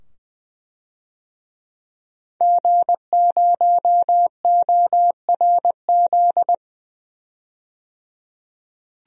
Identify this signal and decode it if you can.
Morse code, heard as a keyed tone, about 20 wpm, 700 Hz. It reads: G0ORZ
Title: DXCC-100 Call Signs - 5 Characters - 20wpm